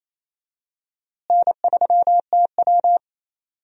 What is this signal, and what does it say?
Morse code, heard as a keyed tone, about 28 wpm, 700 Hz. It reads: N3TW
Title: USA Call Signs - 4 Characters - Repeated 3x - 28wpm